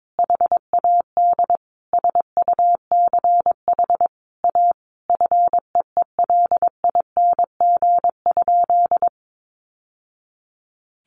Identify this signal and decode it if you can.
Morse code, heard as a keyed tone, about 22 wpm, 700 Hz. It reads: HAD SUCH A FEELING?
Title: Sentences from Top 1200 Words 22wpm